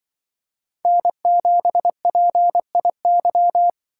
Morse code, heard as a keyed tone, about 24 wpm, 700 Hz. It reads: N7PIY
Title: Speed Racing - USA 1x3 Call Signs 20wpm